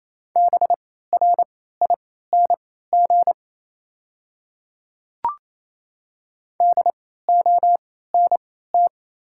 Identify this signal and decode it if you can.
Morse code, heard as a keyed tone, about 28 wpm, spaced as for 16 wpm, 700 Hz. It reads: BRING DONT
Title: 3x Character Spacing - Top 500 Words - 28wpm